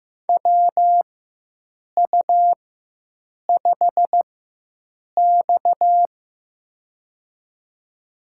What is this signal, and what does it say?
Morse code, heard as a keyed tone, about 15 wpm, spaced as for 7 wpm, 700 Hz. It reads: WU5X
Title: USA Call Signs - 4 Characters - 4x Character Spacing - 15wpm